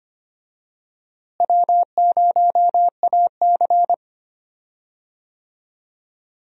Morse code, heard as a keyed tone, about 25 wpm, 700 Hz. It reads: W0AC
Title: USA Call Signs - 4 Characters - Repeated 2x - 25wpm